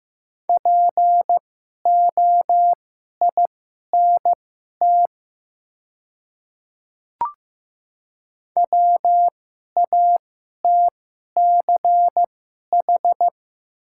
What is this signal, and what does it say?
Morse code, heard as a keyed tone, about 15 wpm, spaced as for 11 wpm, 700 Hz. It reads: POINT WATCH
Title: 2x Character Spacing - Top 500 Words - 15wpm